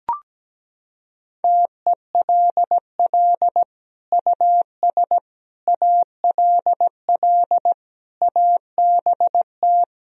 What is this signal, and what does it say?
Morse code, heard as a keyed tone, about 17 wpm, 700 Hz. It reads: TELL US ALL ABT
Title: Sentences from Top 200 Words and Abbreviations 17wpm